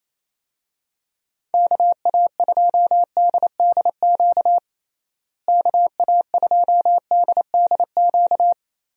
Morse code, heard as a keyed tone, about 28 wpm, 700 Hz. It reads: KA2DDQ KA2DDQ
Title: USA Call Signs - 6 Characters - Repeated 3x - 28wpm